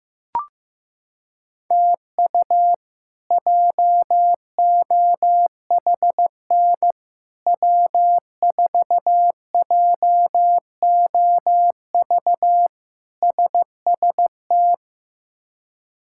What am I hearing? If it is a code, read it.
Morse code, heard as a keyed tone, about 16 wpm, 700 Hz. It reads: TU JOHN W4JOV SST
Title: K1USN SST QSO Practice - 15wpm